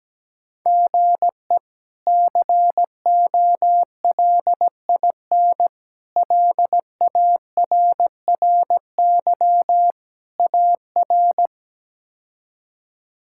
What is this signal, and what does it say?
Morse code, heard as a keyed tone, about 17 wpm, 700 Hz. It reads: GE COLIN LARRY AR
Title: K1USN SST QSO Practice - 17wpm